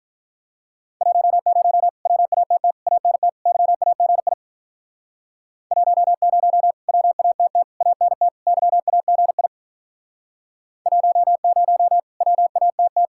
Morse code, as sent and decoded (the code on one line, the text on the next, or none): .---- ----- / .-- .- - - / .- -. - / -.-- .- --. .. / .---- ----- / .-- .- - - / .- -. - / -.-- .- --. .. / .---- ----- / .-- .- - -
10 WATT ANT YAGI 10 WATT ANT YAGI 10 WATT